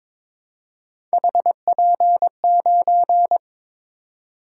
Morse code, heard as a keyed tone, about 21 wpm, 700 Hz. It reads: HP9
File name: practice / 22wpm / Three Letter-Number Combinations - 22wpm